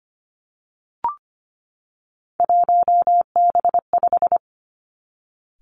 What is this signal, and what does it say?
Morse code, heard as a keyed tone, about 25 wpm, 700 Hz. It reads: 1B5
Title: Three Challenging Letter-Number Combinations 25wpm